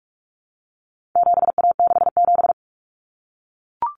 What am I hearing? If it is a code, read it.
Morse code, heard as a keyed tone, about 44 wpm, 700 Hz. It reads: 7A67
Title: Common Cut Numbers - 4-Digit 45wpm